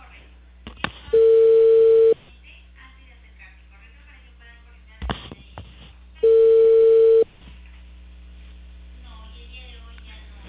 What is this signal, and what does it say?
Ringback tone (China, 450 Hz), with speech.